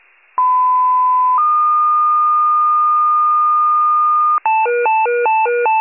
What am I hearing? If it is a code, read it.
Two-tone page, heard as a one-second tone, then a three-second tone, about 1010 then 1220 Hz. It is heard from 0.4 to 4.4 s.